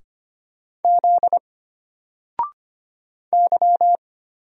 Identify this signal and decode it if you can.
Morse code, heard as a keyed tone, about 25 wpm, 700 Hz. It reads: Z Y
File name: Single-Letter-Number-25wpm